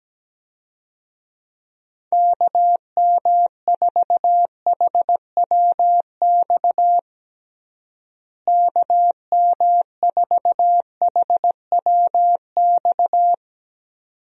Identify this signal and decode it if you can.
Morse code, heard as a keyed tone, about 17 wpm, 700 Hz. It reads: KM4HWX KM4HWX
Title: USA Call Signs - 6 Characters - Repeated 2x - 17wpm